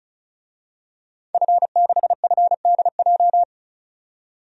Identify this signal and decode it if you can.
Morse code, heard as a keyed tone, about 36 wpm, 700 Hz. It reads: F6FDJ
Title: DXCC-100 Call Signs - 5 Characters - 35wpm